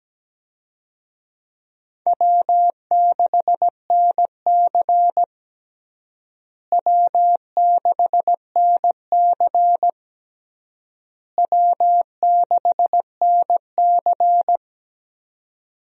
Morse code, heard as a keyed tone, about 17 wpm, 700 Hz. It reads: W6NC W6NC W6NC
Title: USA Call Signs - 4 Characters - Repeated 3x - 17wpm